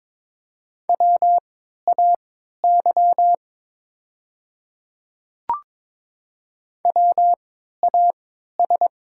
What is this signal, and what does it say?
Morse code, heard as a keyed tone, about 22 wpm, spaced as for 12 wpm, 700 Hz. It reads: WAY WAS
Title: 3x Character Spacing - Top 100 Words - 22wpm